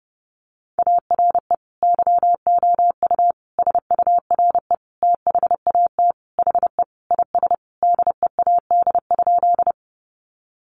Morse code, heard as a keyed tone, about 30 wpm, 700 Hz. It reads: ARE YOU SURE THAT HE IS DEAD?
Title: Sentences from Top 1500 Words 30wpm